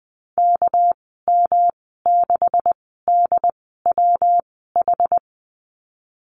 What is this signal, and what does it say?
Morse code, heard as a keyed tone, about 20 wpm, spaced as for 14 wpm, 700 Hz. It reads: KM6DWH
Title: 238 Lesson 38g - Letter q - Call Signs 20wpm